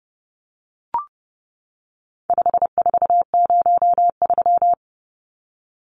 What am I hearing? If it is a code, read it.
Morse code, heard as a keyed tone, about 30 wpm, 700 Hz. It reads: 5403
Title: Numbers - 4-Digit 30wpm